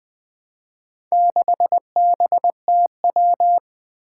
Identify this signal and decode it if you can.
Morse code, heard as a keyed tone, about 20 wpm, 700 Hz. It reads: 6BTW
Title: Four Challenging Letter-Number Combinations 20wpm